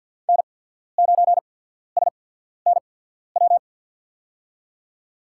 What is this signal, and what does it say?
Morse code, heard as a keyed tone, about 54 wpm, spaced as for 14 wpm, 700 Hz. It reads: N9SNW
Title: USA Call Signs - 5 Characters - 8x Character Spacing - 50wpm